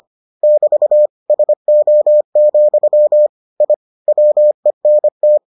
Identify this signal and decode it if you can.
Morse code, heard as a keyed tone, about 25 wpm, 600 Hz. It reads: <BT> SO, I WENT